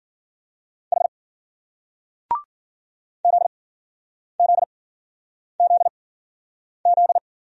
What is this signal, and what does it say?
Morse code, heard as a keyed tone, about 55 wpm, spaced as for 9 wpm, 700 Hz. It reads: H ZZZZ